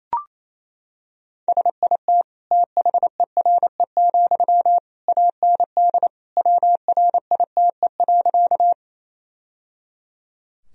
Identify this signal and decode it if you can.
Morse code, heard as a keyed tone, about 28 wpm, 700 Hz. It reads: SIT THERE, AND WRITE.
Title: Sentences from Top 1000 Words - Review Part 1 - 28wpm